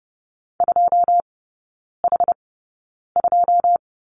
Morse code, heard as a keyed tone, about 30 wpm, spaced as for 9 wpm, 700 Hz. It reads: 2H2
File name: 081 Lesson 14e - Number 2 - Three Characters 30wpm